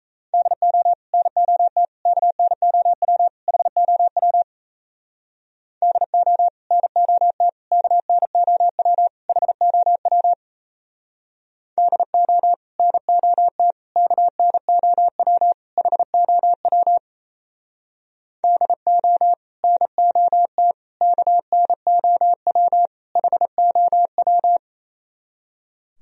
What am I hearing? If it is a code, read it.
Morse code, heard as a keyed tone, about 36 wpm, 700 Hz. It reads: DO NOT KNOW HOW DO NOT KNOW HOW DO NOT KNOW HOW DO NOT KNOW HOW